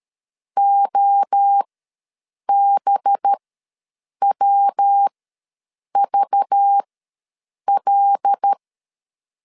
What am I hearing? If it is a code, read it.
Morse code, heard as a keyed tone, about 13 wpm, spaced as for 7 wpm, 800 Hz. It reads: OBWVL
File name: LEZIONI / 5 LBVC / 5 LBV ws 12wpm 3015